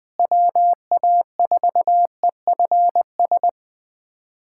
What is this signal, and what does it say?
Morse code, heard as a keyed tone, about 21 wpm, 700 Hz. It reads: WA4EFS
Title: USA Callsign DE Callsign - 6 Characters - 20wpm